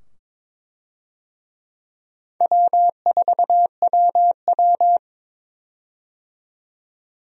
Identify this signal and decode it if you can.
Morse code, heard as a keyed tone, about 22 wpm, 700 Hz. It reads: W4WW
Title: USA Call Signs - 4 Characters - 22wpm